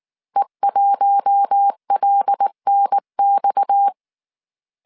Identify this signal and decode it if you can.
Morse code, heard as a keyed tone, about 19 wpm, 800 Hz. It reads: E1LNX
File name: CW / DET / 1 6 1500C S C 18wpm 2356